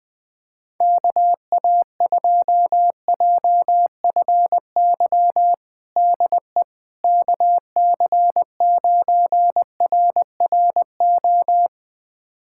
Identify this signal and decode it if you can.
Morse code, heard as a keyed tone, about 20 wpm, 700 Hz. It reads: KA2JFY DE KC9RRO
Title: USA Callsign DE Callsign - 6 Characters - 20wpm